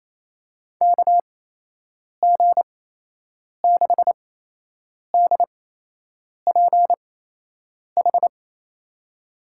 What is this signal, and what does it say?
Morse code, heard as a keyed tone, about 27 wpm, spaced as for 8 wpm, 700 Hz. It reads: KG6DPH
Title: USA Call Signs - 6 Characters - 8x Character Spacing - 28wpm